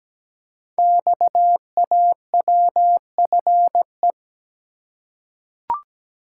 Morse code, heard as a keyed tone, about 17 wpm, 700 Hz. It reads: XAWFE
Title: Code Groups - 5 Letters - 17wpm